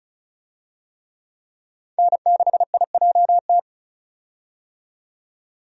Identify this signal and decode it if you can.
Morse code, heard as a keyed tone, about 34 wpm, 700 Hz. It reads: N6IJT
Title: USA Call Signs - 5 Characters - 35wpm